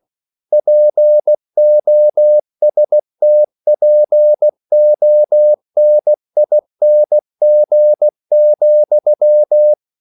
Morse code, heard as a keyed tone, about 16 wpm, 600 Hz. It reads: POSTPONING,
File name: CW / Practice / EnglishQuotations0216